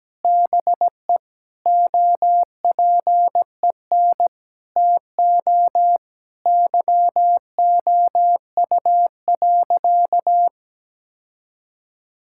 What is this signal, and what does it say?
Morse code, heard as a keyed tone, about 17 wpm, 700 Hz. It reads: BE OPEN TO YOU.
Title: Sentences from Top 1600 Words 17wpm